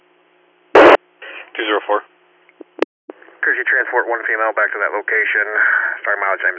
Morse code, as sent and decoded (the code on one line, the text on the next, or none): none